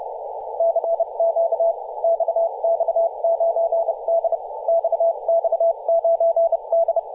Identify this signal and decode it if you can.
Morse code, heard as a keyed tone, about 30 wpm, spaced as for 26 wpm, 670 Hz, with radio noise. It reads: CQ XX9D XX9D
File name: posts / 2017-feb-16-0552z-xx9d-21-026mhz-cw